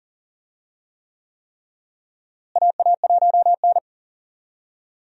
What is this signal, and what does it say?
Morse code, heard as a keyed tone, about 40 wpm, 700 Hz. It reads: AA1N
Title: USA Call Signs - 4 Characters - Repeated 3x - 40wpm